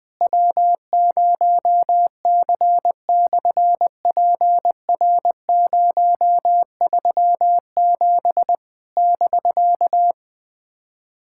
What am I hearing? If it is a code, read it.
Morse code, heard as a keyed tone, about 20 wpm, 700 Hz. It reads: W0C/PR037 <BK>
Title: SOTA QSO Practice 20wpm